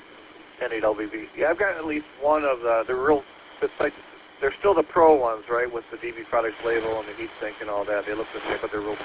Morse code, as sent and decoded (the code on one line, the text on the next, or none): none